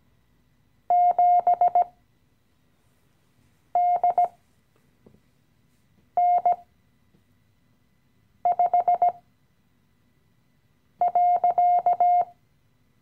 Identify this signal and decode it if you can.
Morse code, heard as a keyed tone, about 17 wpm, spaced as for 4 wpm, 710 Hz. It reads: 7DN5.
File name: CW-Online-Kurs / Anfänger / Lektion 015